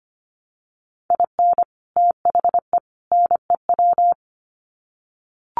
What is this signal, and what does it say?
Morse code, heard as a keyed tone, about 25 wpm, 700 Hz. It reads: IN THE NEW
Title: Sets of 3 Words 25wpm